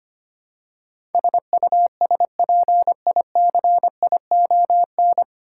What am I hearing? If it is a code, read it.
Morse code, heard as a keyed tone, about 25 wpm, 700 Hz. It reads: SUSPICION
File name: Nine Letter Words - 25wpm